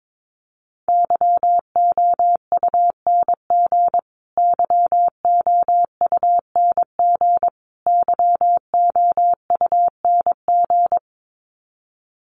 Morse code, heard as a keyed tone, about 22 wpm, 700 Hz. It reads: YOUNG YOUNG YOUNG
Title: Top 500 Words Repeated 22wpm